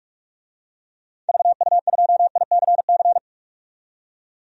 Morse code, heard as a keyed tone, about 47 wpm, 700 Hz. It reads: VU2ICC
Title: DXCC-100 Call Signs - 6 Characters - 45wpm